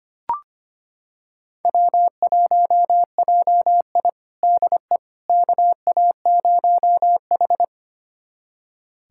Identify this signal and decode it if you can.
Morse code, heard as a keyed tone, about 25 wpm, 700 Hz. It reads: W1JI DE KA0H
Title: USA Callsign DE Callsign - 4 Characters - 25wpm